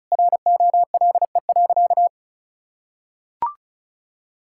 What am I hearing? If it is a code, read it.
Morse code, heard as a keyed tone, about 36 wpm, 700 Hz. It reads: ROLE.
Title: Sentences from Top 1000 Words - Review Part 6 - 35wpm